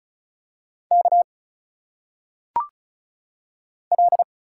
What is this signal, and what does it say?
Morse code, heard as a keyed tone, about 34 wpm, 700 Hz. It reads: K L
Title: Single Letters - Phonetic Pronunciation 35wpm